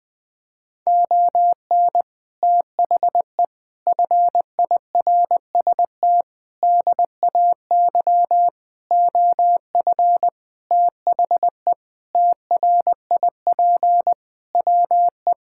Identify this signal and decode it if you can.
Morse code, heard as a keyed tone, about 21 wpm, 700 Hz. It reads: ON THE FIRST DAY OF THE TRIP WE
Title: Sentences with 6-Word Phrases - Spoken Beginning - 20wpm